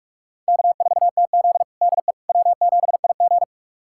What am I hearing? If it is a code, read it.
Morse code, heard as a keyed tone, about 48 wpm, 700 Hz. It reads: K4TZ DE W7IG
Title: USA Callsign DE Callsign - 4 Characters - 45wpm